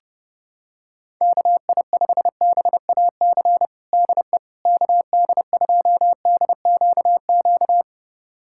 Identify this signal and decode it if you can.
Morse code, heard as a keyed tone, about 30 wpm, 700 Hz. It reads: KI5BAC DE KD2DQQ
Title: USA Callsign DE Callsign - 6 Characters - 30wpm